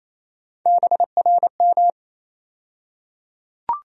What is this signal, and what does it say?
Morse code, heard as a keyed tone, about 28 wpm, 700 Hz. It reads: BRM
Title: USA Call Sign Postfixes - 3 Letters 28wpm